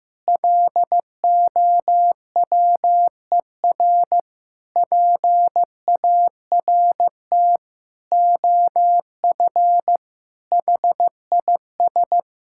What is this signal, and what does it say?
Morse code, heard as a keyed tone, about 16 wpm, 700 Hz. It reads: LOWER PART OF HIS
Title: Sentences from Top 1000 Words - Review Part 5 - 15wpm